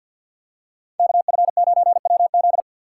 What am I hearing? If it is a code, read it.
Morse code, heard as a keyed tone, about 52 wpm, 700 Hz. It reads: KF9WZ
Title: USA Call Signs - 2x Character Spacing - 50wpm